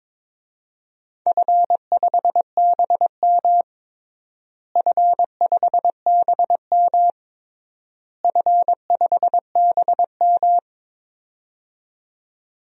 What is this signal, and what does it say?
Morse code, heard as a keyed tone, about 22 wpm, 700 Hz. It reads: F5BM F5BM F5BM
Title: DXCC-100 Call Signs - 4 Characters - Repeated 3x - 22wpm